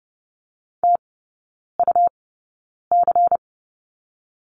Morse code, heard as a keyed tone, about 30 wpm, spaced as for 9 wpm, 700 Hz. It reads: TUC
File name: Lesson 19e - Letter w - Three Characters 30wpm